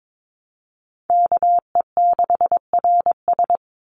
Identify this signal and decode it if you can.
Morse code, heard as a keyed tone, about 22 wpm, 700 Hz. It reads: KE6RS
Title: NA SOTA Chaser Callsigns 22wpm